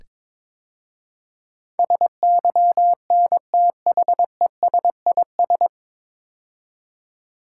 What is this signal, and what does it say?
Morse code, heard as a keyed tone, about 22 wpm, 700 Hz. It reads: SYNTHESIS
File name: 9 Letter Words - First 3 Letters Given - 22wpm